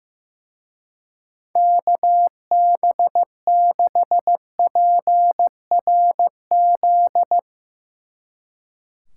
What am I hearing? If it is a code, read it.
Morse code, heard as a keyed tone, about 15 wpm, 700 Hz. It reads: KB6PRZ
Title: Speed Racing - USA 2x3 Call Signs 15wpm